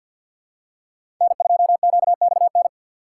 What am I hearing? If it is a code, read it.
Morse code, heard as a keyed tone, about 53 wpm, 700 Hz. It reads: N2QXN
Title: USA Call Signs - 5 Characters - 2x Character Spacing - 50wpm